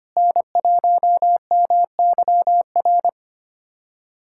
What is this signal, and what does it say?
Morse code, heard as a keyed tone, about 25 wpm, 700 Hz. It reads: N1MYR
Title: USA Callsign DE Callsign 25wpm